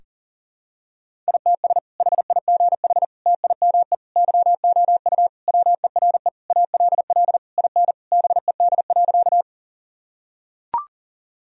Morse code, heard as a keyed tone, about 40 wpm, 700 Hz. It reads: ITS HIGH TIME YOU WERE ALL IN BED.
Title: Sentences from Top 1000 Words - Review Part 1 - 40wpm